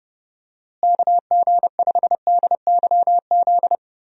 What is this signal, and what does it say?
Morse code, heard as a keyed tone, about 32 wpm, 700 Hz. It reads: KG5DYZ
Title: USA Call Signs - 6 Characters - Repeated 2x - 30wpm